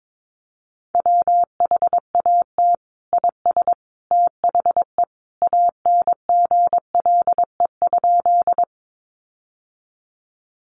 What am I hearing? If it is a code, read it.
Morse code, heard as a keyed tone, about 22 wpm, 700 Hz. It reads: WHAT IS THE ANGLE?
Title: Sentences from Top 1500 Words 22wpm